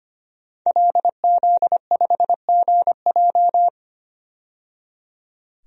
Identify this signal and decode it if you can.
Morse code, heard as a keyed tone, about 25 wpm, 700 Hz. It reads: LZ5GJ
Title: DXCC-100 Call Signs - 5 Characters - 25wpm